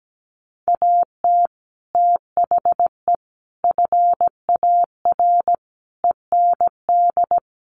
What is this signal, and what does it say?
Morse code, heard as a keyed tone, about 17 wpm, 700 Hz. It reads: AT THE FAR END
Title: Sets of 5 Words 17wpm